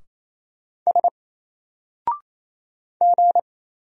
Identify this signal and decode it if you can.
Morse code, heard as a keyed tone, about 28 wpm, 700 Hz. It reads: S G
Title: Single Character (Letter:Number:Punctuation:Prosign) Alt 28wpm